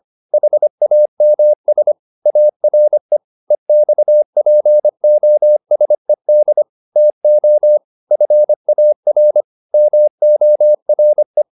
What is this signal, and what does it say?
Morse code, heard as a keyed tone, about 25 wpm, 600 Hz. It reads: HAMS ARE EXPOSED TO FAR MORE